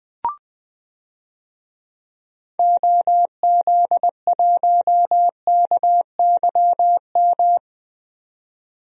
Morse code, heard as a keyed tone, about 21 wpm, 700 Hz. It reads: OZ1KYM
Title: DXCC-100 Call Signs - 6 Characters - 20wpm